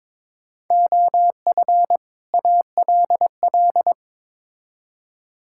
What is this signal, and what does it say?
Morse code, heard as a keyed tone, about 22 wpm, 700 Hz. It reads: OF ALL
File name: Build A Phrase - 2-Word Phrases - 22wpm